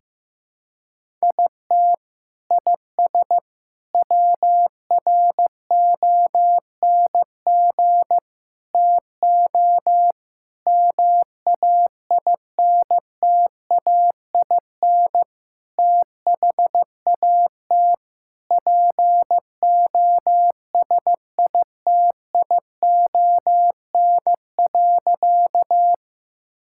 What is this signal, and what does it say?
Morse code, heard as a keyed tone, about 15 wpm, 700 Hz. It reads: IT IS WRONG TO MAINTAIN THAT POSITION.
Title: Sentences from Top 1000 Words - Review Part 5 - 15wpm